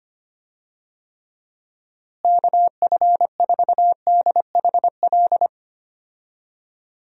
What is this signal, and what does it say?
Morse code, heard as a keyed tone, about 25 wpm, 700 Hz. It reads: KF4DHL
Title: USA Call Signs - 6 Characters - Repeated 2x - 25wpm